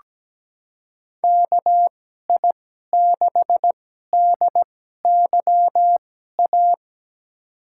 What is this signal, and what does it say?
Morse code, heard as a keyed tone, about 17 wpm, spaced as for 12 wpm, 700 Hz. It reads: KI6DYA
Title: USA Call Signs - 6 Characters - 2x Character Spacing - 17wpm